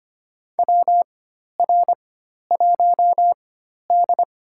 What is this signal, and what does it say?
Morse code, heard as a keyed tone, about 25 wpm, spaced as for 12 wpm, 700 Hz. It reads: WR1D
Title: USA Call Signs - 4 Characters - 4x Character Spacing - 25wpm